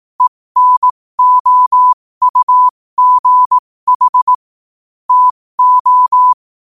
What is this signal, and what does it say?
Morse code, heard as a keyed tone, about 18 wpm, spaced as for 15 wpm, 1000 Hz. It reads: ENOUGH TO
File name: qotdfiles / 15wpm